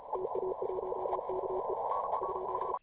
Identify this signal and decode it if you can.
Morse code, heard as a keyed tone, about 34 wpm, 380 Hz, with radio noise. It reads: TA1KE ?